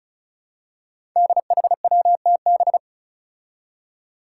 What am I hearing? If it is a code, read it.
Morse code, heard as a keyed tone, about 36 wpm, 700 Hz. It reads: DHWTB
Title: Five Challenging Letter-Number Combinations 35wpm